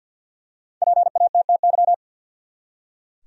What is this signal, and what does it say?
Morse code, heard as a keyed tone, about 47 wpm, 700 Hz. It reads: PATTY
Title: USA Common Names - 50wpm